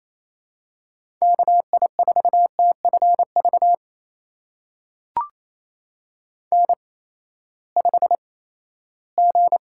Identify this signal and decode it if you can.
Morse code, heard as a keyed tone, about 27 wpm, spaced as for 16 wpm, 700 Hz. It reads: KI4TFV N5G
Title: USA Call Signs - 8x Character Spacing - 28wpm